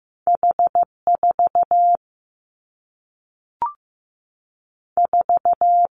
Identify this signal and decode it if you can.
Morse code, heard as a keyed tone, about 15 wpm, 700 Hz. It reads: H 4 4
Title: Two Challenging Letter-Number Combinations 15wpm